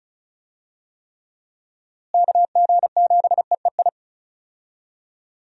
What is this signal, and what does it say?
Morse code, heard as a keyed tone, about 34 wpm, 700 Hz. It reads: KG7EEI
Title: USA Call Signs - 6 Characters - Repeated 2x - 35wpm